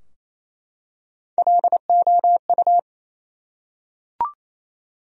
Morse code, heard as a keyed tone, about 27 wpm, 700 Hz. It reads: LOU
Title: USA Common Names - Max 4 Letters - 28wpm